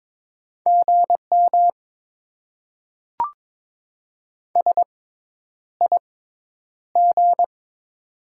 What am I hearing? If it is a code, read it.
Morse code, heard as a keyed tone, about 22 wpm, spaced as for 10 wpm, 700 Hz. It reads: GM SIG
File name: Top 90 QSO Elements - 6x Character Spacing - 22wpm